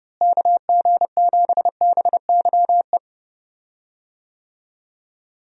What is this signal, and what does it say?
Morse code, heard as a keyed tone, about 32 wpm, 700 Hz. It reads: KG7BYE
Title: USA Call Signs - 6 Characters - Repeated 2x - 30wpm